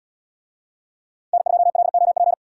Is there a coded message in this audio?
Yes: Morse code from 1.3 to 2.3 s, heard as a keyed tone, about 75 wpm, 700 Hz. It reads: N2LOP